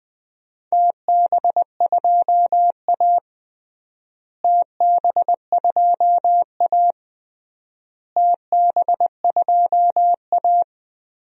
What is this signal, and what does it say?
Morse code, heard as a keyed tone, about 21 wpm, 700 Hz. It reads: TB2A TB2A TB2A
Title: DXCC-100 Call Signs - 4 Characters - Repeated 3x - 20wpm